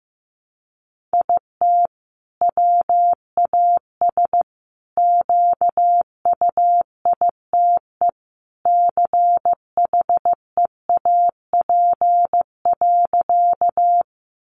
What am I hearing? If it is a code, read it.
Morse code, heard as a keyed tone, about 15 wpm, 700 Hz. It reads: IT WAS QUITE CHEAP.
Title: Sentences from Top 1200 Words 15wpm